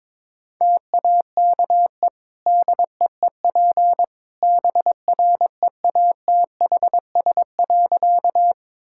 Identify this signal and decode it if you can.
Morse code, heard as a keyed tone, about 22 wpm, 700 Hz. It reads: TAKE DEEP BREATHS.
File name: practice / 22wpm / Easy Sentences with Hints - 22wpm